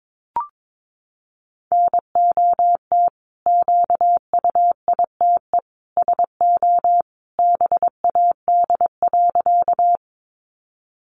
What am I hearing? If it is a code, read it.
Morse code, heard as a keyed tone, about 22 wpm, 700 Hz. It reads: NOT QUITE SO BAD.
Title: Sentences from Top 700 Words 22wpm